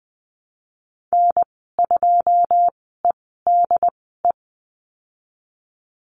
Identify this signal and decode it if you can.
Morse code, heard as a keyed tone, about 20 wpm, spaced as for 14 wpm, 700 Hz. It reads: N2EDE
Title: 088 Lesson 15g - Number 5 - Call Signs 20wpm